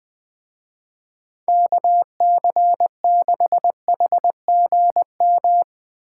Morse code, heard as a keyed tone, about 20 wpm, 700 Hz. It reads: KC6HGM